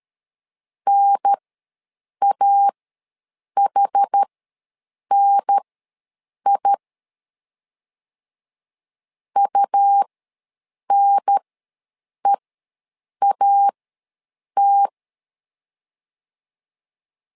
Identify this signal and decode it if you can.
Morse code, heard as a keyed tone, about 13 wpm, spaced as for 7 wpm, 800 Hz. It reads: NAHNI UNEAT
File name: LEZIONI / 2 UHDN / 2 UHDN ws 12wpm 1708